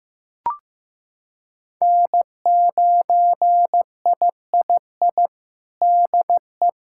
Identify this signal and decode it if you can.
Morse code, heard as a keyed tone, about 15 wpm, 700 Hz. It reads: N9III DE
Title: Practice QSO - Exchange 4 - Ending - 15wpm